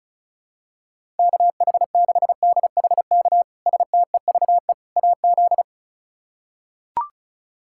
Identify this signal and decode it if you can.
Morse code, heard as a keyed tone, about 36 wpm, 700 Hz. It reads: KH6DHK STEVE AZ
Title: K1USN SST QSO Practice - 35wpm